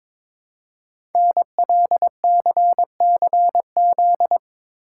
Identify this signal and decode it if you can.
Morse code, heard as a keyed tone, about 22 wpm, 700 Hz. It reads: NLCCZ
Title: Code Groups - 5 Letters - 22wpm